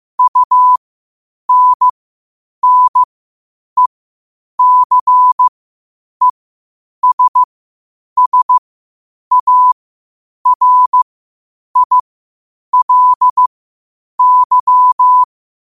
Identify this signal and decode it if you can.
Morse code, heard as a keyed tone, about 15 wpm, spaced as for 8 wpm, 1000 Hz. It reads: UNNECESSARILY
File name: qotdfiles / 8wpm